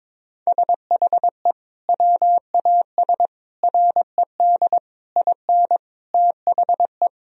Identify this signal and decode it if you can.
Morse code, heard as a keyed tone, about 22 wpm, 700 Hz. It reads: SHE WAS RED IN THE
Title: Sentences from Top 1000 Words - Review Part 2 - 22wpm